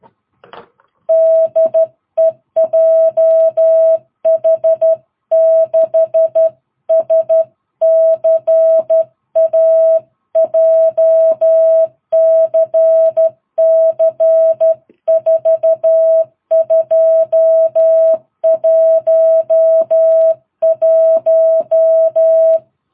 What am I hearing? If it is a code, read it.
Morse code, heard as a keyed tone, about 12 wpm, 660 Hz. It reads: DE JH6SCA JCC 4211
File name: jh8cbh / 2022cw